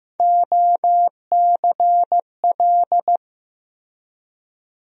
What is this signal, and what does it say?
Morse code, heard as a keyed tone, about 15 wpm, 700 Hz. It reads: OCL
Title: USA Call Sign Postfixes - 3 Letters 15wpm